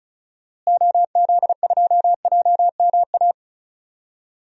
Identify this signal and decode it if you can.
Morse code, heard as a keyed tone, about 34 wpm, 700 Hz. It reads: OZ2JMA
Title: DXCC-100 Call Signs - 6 Characters - Repeated 3x - 35wpm